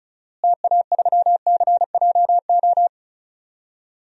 Morse code, heard as a keyed tone, about 36 wpm, 700 Hz. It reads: TA3CJO
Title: DXCC-100 Call Signs - 6 Characters - Repeated 3x - 35wpm